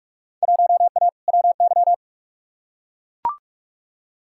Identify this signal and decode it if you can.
Morse code, heard as a keyed tone, about 44 wpm, 700 Hz. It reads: 1A WY
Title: Field Day Exchanges - 45wpm